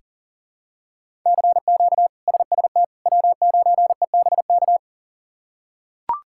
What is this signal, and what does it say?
Morse code, heard as a keyed tone, about 37 wpm, 700 Hz. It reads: CQ SST W9EBK
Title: K1USN SST QSO Practice - 40wpm